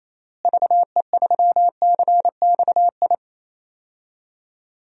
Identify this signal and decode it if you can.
Morse code, heard as a keyed tone, about 28 wpm, 700 Hz. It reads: VE3CXI
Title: DXCC-100 Call Signs - 6 Characters - Repeated 3x - 28wpm